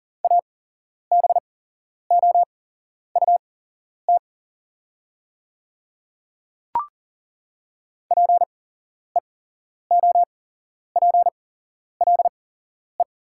Morse code, heard as a keyed tone, about 40 wpm, spaced as for 11 wpm, 700 Hz. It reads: ABOUT PEOPLE
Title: 8x Character Spacing - Top 100 Words - 40wpm